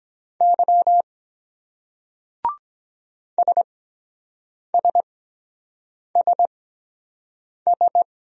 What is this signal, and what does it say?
Morse code, heard as a keyed tone, about 23 wpm, spaced as for 7 wpm, 700 Hz. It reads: Y SSSS